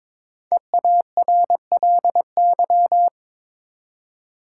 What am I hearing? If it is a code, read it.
Morse code, heard as a keyed tone, about 22 wpm, 700 Hz. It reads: EARLY
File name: 5 Letter Words - First 3 Letters Given - 22wpm